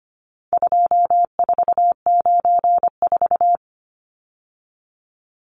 Morse code, heard as a keyed tone, about 25 wpm, 700 Hz. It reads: 2494
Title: Numbers - 4-Digit 25wpm